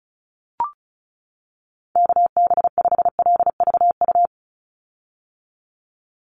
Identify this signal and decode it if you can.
Morse code, heard as a keyed tone, about 34 wpm, 700 Hz. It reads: KB5LVU
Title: USA Call Signs 35wpm